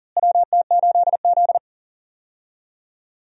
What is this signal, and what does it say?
Morse code, heard as a keyed tone, about 40 wpm, 700 Hz. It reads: WT8Z
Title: USA Call Signs - 4 Characters - Repeated 3x - 40wpm